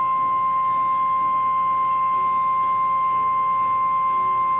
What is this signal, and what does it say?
Emergency Alert System (SAME) tones (the 1050 Hz attention tone). It is heard from 0.4 to 4.4 s.
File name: eas / KNCYKBIE / KNCYKBIE